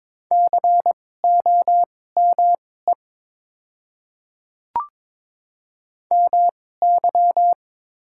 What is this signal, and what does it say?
Morse code, heard as a keyed tone, about 21 wpm, spaced as for 16 wpm, 700 Hz. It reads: COME MY
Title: 2x Character Spacing - Top 100 Words - 22wpm